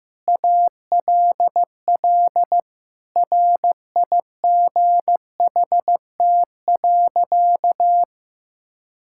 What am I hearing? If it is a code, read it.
Morse code, heard as a keyed tone, about 15 wpm, 700 Hz. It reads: ALL RIGHT.
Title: Sentences from Top 1600 Words 15wpm